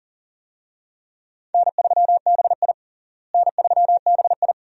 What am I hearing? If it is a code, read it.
Morse code, heard as a keyed tone, about 40 wpm, 700 Hz. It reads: N3BI N3BI
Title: USA Call Signs - 4 Characters - Repeated 2x - 40wpm